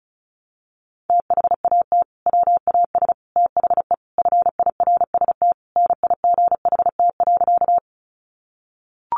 Morse code, heard as a keyed tone, about 34 wpm, 700 Hz. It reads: THAT WAS THE FIRST NIGHT.
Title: Sentences from Top 300 Words 35wpm